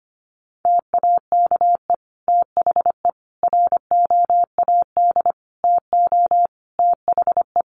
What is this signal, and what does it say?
Morse code, heard as a keyed tone, about 25 wpm, 700 Hz. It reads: TAKE THE ROAD TO THE
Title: Sentences from Top 400 Words 25wpm